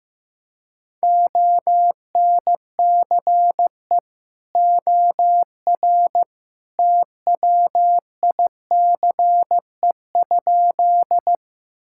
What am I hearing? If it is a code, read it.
Morse code, heard as a keyed tone, about 16 wpm, 700 Hz. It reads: ONCE OR TWICE?
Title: Sentences from Top 1600 Words 15wpm